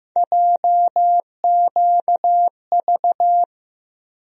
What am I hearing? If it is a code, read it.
Morse code, heard as a keyed tone, about 15 wpm, 700 Hz. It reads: JQV
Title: USA Call Sign Postfixes - 3 Letters 15wpm